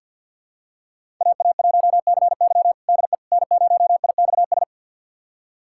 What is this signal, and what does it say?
Morse code, heard as a keyed tone, about 52 wpm, 700 Hz. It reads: AA1CY DE N0IXS
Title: USA Callsign DE Callsign - 5 Characters - 50wpm